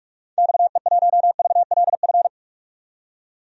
Morse code, heard as a keyed tone, about 44 wpm, 700 Hz. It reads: XE1VLF
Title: DXCC-100 Call Signs - 6 Characters - 45wpm